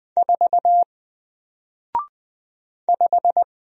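Morse code, heard as a keyed tone, about 20 wpm, 700 Hz. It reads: 4 5
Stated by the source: Single letter and number Morse drill at 20 wpm, from N4LCD.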